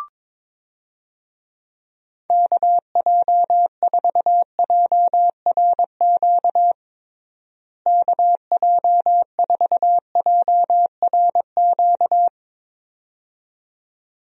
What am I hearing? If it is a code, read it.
Morse code, heard as a keyed tone, about 21 wpm, 700 Hz. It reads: KJ4JRQ KJ4JRQ
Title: USA Call Signs - 6 Characters - Repeated 2x - 22wpm